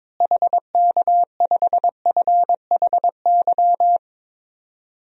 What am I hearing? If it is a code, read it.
Morse code, heard as a keyed tone, about 22 wpm, 700 Hz. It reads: HK5FHY
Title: DXCC-100 Call Signs - 6 Characters - Repeated 2x - 22wpm